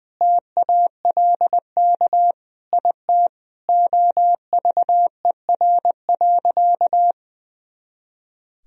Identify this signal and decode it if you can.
Morse code, heard as a keyed tone, about 21 wpm, 700 Hz. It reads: TALK IT OVER.
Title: Sentences from Top 500 Words and Abbreviations 20wpm